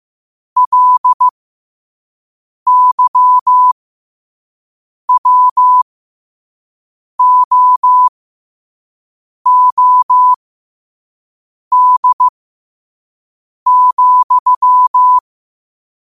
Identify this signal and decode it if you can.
Morse code, heard as a keyed tone, about 15 wpm, spaced as for 5 wpm, 1000 Hz. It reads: LYWOOD,
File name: qotdfiles / 5wpm